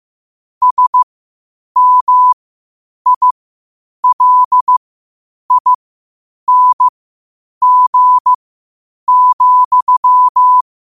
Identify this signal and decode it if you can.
Morse code, heard as a keyed tone, about 15 wpm, spaced as for 8 wpm, 1000 Hz. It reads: SMILING,
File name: qotdfiles / 8wpm